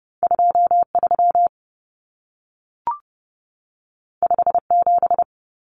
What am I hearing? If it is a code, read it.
Morse code, heard as a keyed tone, about 30 wpm, 700 Hz. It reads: 23 57
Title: Common Cut Numbers - 2-Digit 30wpm